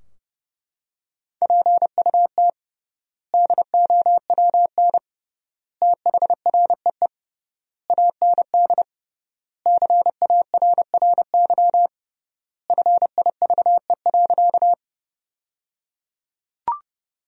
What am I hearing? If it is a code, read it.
Morse code, heard as a keyed tone, about 30 wpm, 700 Hz. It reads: PUT DOWN THREE AND CARRY FIVE.
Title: 3x Word-Spacing - Sentences from Top 400 Words - 30wpm